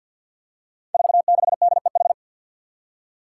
Morse code, heard as a keyed tone, about 51 wpm, 700 Hz. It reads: 46DEH
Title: Five Challenging Letter-Number Combinations 50wpm